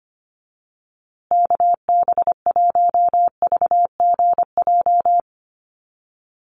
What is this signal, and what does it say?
Morse code, heard as a keyed tone, about 25 wpm, 700 Hz. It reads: KB1VGJ
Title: USA Call Signs Repeated 2x 25wpm